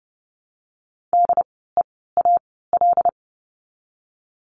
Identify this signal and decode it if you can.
Morse code, heard as a keyed tone, about 30 wpm, 700 Hz. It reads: D E A L
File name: Lesson 16f - Letter c - Words 30wpm